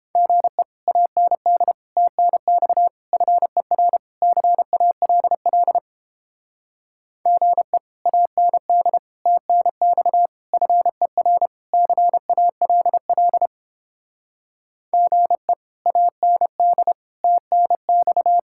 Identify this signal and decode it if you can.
Morse code, heard as a keyed tone, about 29 wpm, 700 Hz. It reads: GE AND TNX FER CALL GE AND TNX FER CALL GE AND TNX